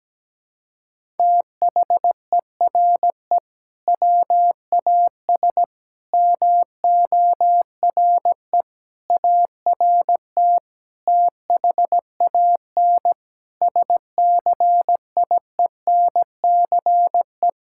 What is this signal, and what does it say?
Morse code, heard as a keyed tone, about 17 wpm, 700 Hz. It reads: THERE WAS MORE ART THAN SCIENCE
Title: Sentences from Top 1000 Words - Review Part 6 - 17wpm